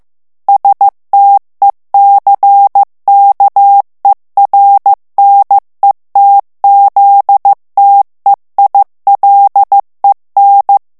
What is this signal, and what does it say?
Morse code, heard as a keyed tone, about 15 wpm, 800 Hz. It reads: STECKERNETZTEILEN
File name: morse mp3 / ALLEMAND texte clair 1 vitesse 14 mots minute